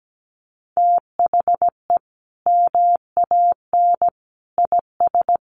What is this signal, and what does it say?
Morse code, heard as a keyed tone, about 17 wpm, 700 Hz. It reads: THE MAN IS
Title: Sentences from Top 1400 Words 17wpm